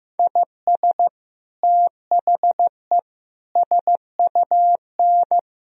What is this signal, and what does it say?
Morse code, heard as a keyed tone, about 15 wpm, 700 Hz. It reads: IS THE SUN
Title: Sentences from Top 1000 Words - Review Part 1 - 15wpm